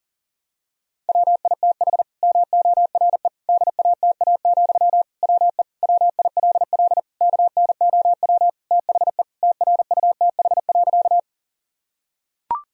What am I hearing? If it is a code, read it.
Morse code, heard as a keyed tone, about 40 wpm, 700 Hz. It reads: WITH MORE DATA, WE WILL KNOW THE TRUTH.
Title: Sentences from Top 1000 Words - Review Part 3 - 40wpm